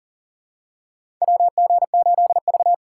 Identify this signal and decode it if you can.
Morse code, heard as a keyed tone, about 40 wpm, 700 Hz. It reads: WG8V
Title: USA Call Signs - 4x Character Spacing - 40wpm